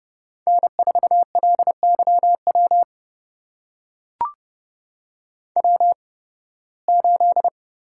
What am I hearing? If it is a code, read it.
Morse code, heard as a keyed tone, about 32 wpm, spaced as for 20 wpm, 700 Hz. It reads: N4LYW W8
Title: USA Call Signs - 5 Characters - 8x Character Spacing - 30wpm